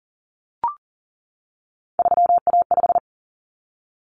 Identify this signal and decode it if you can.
Morse code, heard as a keyed tone, about 40 wpm, 700 Hz. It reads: 3A5
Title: Common Cut Numbers - 3-Digit 40wpm